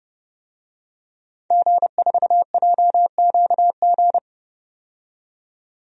Morse code, heard as a keyed tone, about 30 wpm, 700 Hz. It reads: G4JQG
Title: DXCC-100 Call Signs - 5 Characters - 30wpm